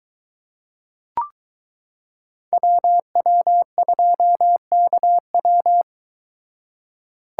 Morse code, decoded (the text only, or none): WW2KW